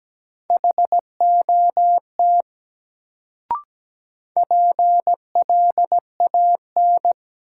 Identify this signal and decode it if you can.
Morse code, heard as a keyed tone, about 17 wpm, 700 Hz. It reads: HOT PLAN
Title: Sentences from Top 500 Words - New Word Drill - 17wpm